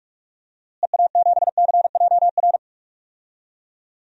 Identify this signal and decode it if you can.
Morse code, heard as a keyed tone, about 46 wpm, 700 Hz. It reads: EA7CJR
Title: DXCC-100 Call Signs - 6 Characters - Repeated 3x - 45wpm